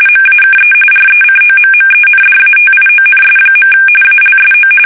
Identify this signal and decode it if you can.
RTTY, about 45.45 baud. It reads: MAKE RECOMMENDATIONS. ALSO
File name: fsk850 / usnavy3